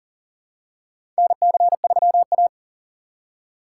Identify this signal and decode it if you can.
Morse code, heard as a keyed tone, about 40 wpm, 700 Hz. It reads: NC3A
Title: USA Call Signs - 4 Characters - 3x Character Spacing - 40wpm